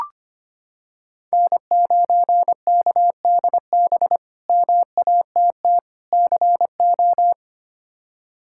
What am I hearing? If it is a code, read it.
Morse code, heard as a keyed tone, about 25 wpm, 700 Hz. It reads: N9KDB MATT CO
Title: K1USN SST QSO Practice - 25wpm